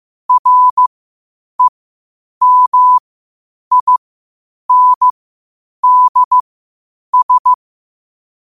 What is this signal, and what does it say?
Morse code, heard as a keyed tone, about 15 wpm, spaced as for 8 wpm, 1000 Hz. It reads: REMINDS
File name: qotdfiles / 8wpm